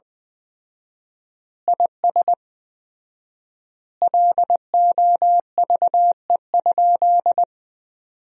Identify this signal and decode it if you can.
Morse code, heard as a keyed tone, about 20 wpm, 700 Hz. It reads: IS LOVE?
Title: 4x Word-Spacing - Sentences from Top 400 Words - 20wpm